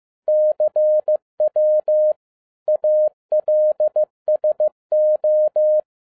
Morse code, heard as a keyed tone, about 15 wpm, 610 Hz. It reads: CW ALSO
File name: EXCITING QRP CW OPERATION 15wpm